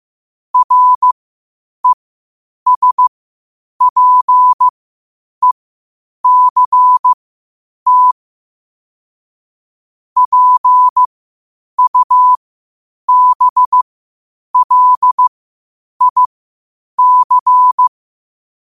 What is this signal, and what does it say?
Morse code, heard as a keyed tone, about 15 wpm, spaced as for 8 wpm, 1000 Hz. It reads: RESPECT PUBLIC
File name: qotdfiles / 8wpm